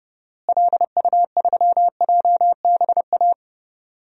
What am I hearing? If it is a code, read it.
Morse code, heard as a keyed tone, about 31 wpm, 700 Hz. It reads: LU3JBA
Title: DXCC-100 Call Signs - 6 Characters - Repeated 2x - 30wpm